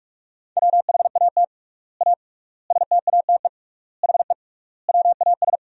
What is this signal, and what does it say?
Morse code, heard as a keyed tone, about 43 wpm, 700 Hz. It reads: WHAT A STATE HE WAS
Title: 3x Word-Spacing - Sentences from Top 400 Words - 45wpm